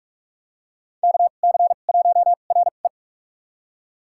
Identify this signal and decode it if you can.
Morse code, heard as a keyed tone, about 46 wpm, spaced as for 32 wpm, 700 Hz. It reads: KC1RE
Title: USA Call Signs - 2x Character Spacing - 45wpm